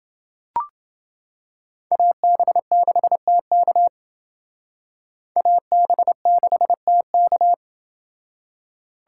Morse code, heard as a keyed tone, about 28 wpm, 700 Hz. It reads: AB6TK AB6TK